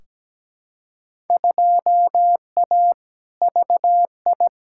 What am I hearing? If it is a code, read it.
Morse code, heard as a keyed tone, about 17 wpm, 700 Hz. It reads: 2A VI
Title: Field Day Exchanges Encore - 17wpm